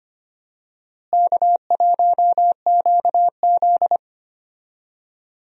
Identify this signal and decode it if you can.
Morse code, heard as a keyed tone, about 25 wpm, 700 Hz. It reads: K1QZ
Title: USA Call Signs - 4 Characters - Repeated 2x - 25wpm